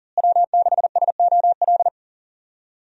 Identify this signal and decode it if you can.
Morse code, heard as a keyed tone, about 40 wpm, 700 Hz. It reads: W6SOL
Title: USA Call Signs - 5 Characters - Repeated 3x - 40wpm